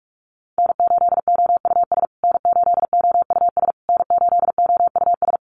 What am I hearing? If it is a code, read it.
Morse code, heard as a keyed tone, about 44 wpm, 700 Hz. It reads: N8OUS N8OUS N8OUS
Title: USA Call Signs Repeated 3x 45wpm